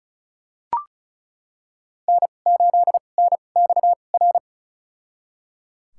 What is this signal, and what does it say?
Morse code, heard as a keyed tone, about 36 wpm, spaced as for 25 wpm, 700 Hz. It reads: N8NXR
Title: USA Call Signs - 5 Characters - 2x Character Spacing - 35wpm